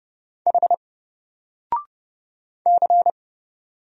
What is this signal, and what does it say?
Morse code, heard as a keyed tone, about 29 wpm, 700 Hz. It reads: H C
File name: Single-Letter-Number-30wpm